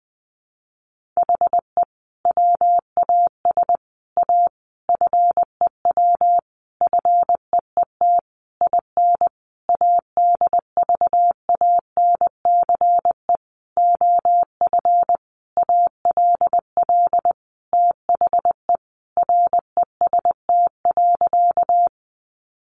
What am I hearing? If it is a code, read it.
Morse code, heard as a keyed tone, about 20 wpm, 700 Hz. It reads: HE WAS A FEW FEET IN ADVANCE OF ALL THE REST.
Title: Sentences from Top 1400 Words 20wpm